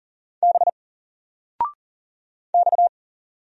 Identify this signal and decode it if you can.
Morse code, heard as a keyed tone, about 40 wpm, 700 Hz. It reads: B X
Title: Single Letter-Number 40wpm